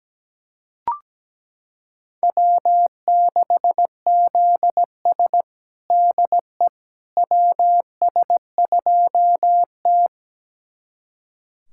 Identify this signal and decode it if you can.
Morse code, heard as a keyed tone, about 17 wpm, 700 Hz. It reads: W6ZS DE WS2T
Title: USA Callsign DE Callsign - 4 Characters - 17wpm